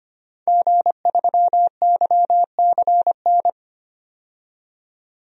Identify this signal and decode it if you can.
Morse code, heard as a keyed tone, about 25 wpm, 700 Hz. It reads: G3YCN
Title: DXCC-100 Call Signs - 5 Characters - 25wpm